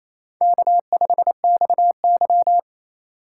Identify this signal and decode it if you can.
Morse code, heard as a keyed tone, about 28 wpm, 700 Hz. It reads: K5XY
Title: USA Call Signs - 4 Characters - 4x Character Spacing - 28wpm